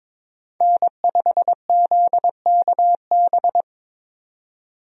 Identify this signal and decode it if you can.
Morse code, heard as a keyed tone, about 22 wpm, 700 Hz. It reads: N5ZKB
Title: USA Call Signs - 5 Characters - 22wpm